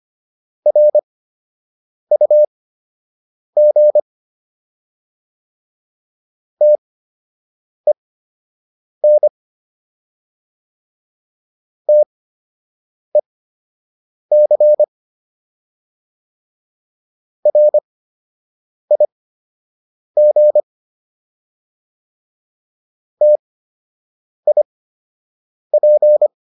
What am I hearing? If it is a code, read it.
Morse code, heard as a keyed tone, about 25 wpm, spaced as for 7 wpm, 600 Hz. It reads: RUG TEN TEC RIG TIP